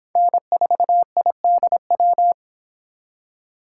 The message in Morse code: N4IDW

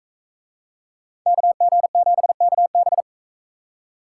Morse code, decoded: KG7KB

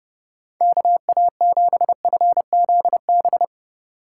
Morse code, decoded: KA7FZB

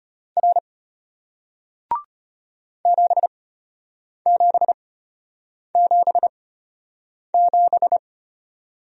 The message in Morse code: R 7777

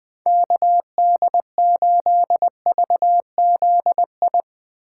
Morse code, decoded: KD8VZI

